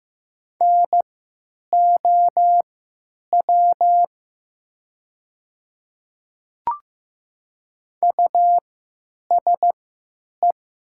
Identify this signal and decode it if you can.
Morse code, heard as a keyed tone, about 15 wpm, spaced as for 8 wpm, 700 Hz. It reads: NOW USE
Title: 3x Character Spacing - Top 200 Words - 15wpm